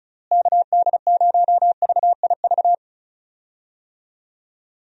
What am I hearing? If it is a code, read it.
Morse code, heard as a keyed tone, about 34 wpm, 700 Hz. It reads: KD0VIV
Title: USA Call Signs - 6 Characters - Repeated 3x - 35wpm